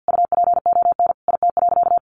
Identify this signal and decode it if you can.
Morse code, heard as a keyed tone, about 50 wpm, 700 Hz. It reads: UPON IT.
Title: Sentences from Top 1500 Words 50wpm